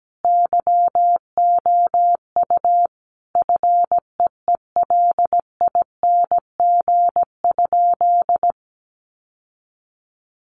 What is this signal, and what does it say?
Morse code, heard as a keyed tone, about 17 wpm, 700 Hz. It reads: YOU FEELING?
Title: Sentences from Top 1200 Words 17wpm